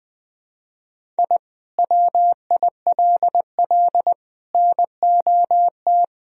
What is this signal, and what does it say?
Morse code, heard as a keyed tone, about 20 wpm, 700 Hz. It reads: I WILL NOT
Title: Sentences from Top 1000 Words - Review Part 1 - 20wpm